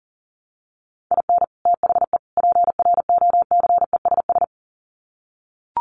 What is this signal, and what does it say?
Morse code, heard as a keyed tone, about 40 wpm, 700 Hz. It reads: IN THE PROCESS
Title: Sets of 3 Words 40wpm